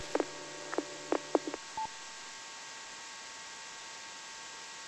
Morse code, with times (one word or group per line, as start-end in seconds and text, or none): none